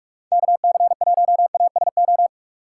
Morse code, heard as a keyed tone, about 41 wpm, 700 Hz. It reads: KC1ASO
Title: USA Call Signs - 6 Characters - 6x Character Spacing - 45wpm